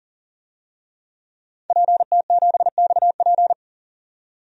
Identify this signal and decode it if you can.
Morse code, heard as a keyed tone, about 39 wpm, 700 Hz. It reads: PT7XP
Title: DXCC-100 Call Signs - 5 Characters - 40wpm